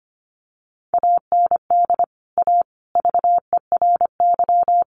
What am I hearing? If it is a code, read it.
Morse code, heard as a keyed tone, about 25 wpm, 700 Hz. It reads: AND A VERY
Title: Sentences from Top 200 Words 25wpm